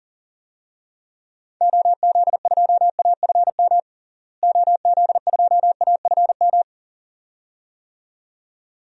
Morse code, heard as a keyed tone, about 40 wpm, 700 Hz. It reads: OZ2AFM OZ2AFM
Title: DXCC-100 Call Signs - 6 Characters - Repeated 2x - 40wpm